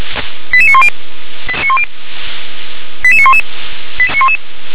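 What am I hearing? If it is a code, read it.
Five-tone selective call (ZVEI), heard as quick runs of five tones, tones of about 70 ms. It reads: 88910 88911
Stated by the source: Five-tone ZVEI selective call 88152 received by SDR.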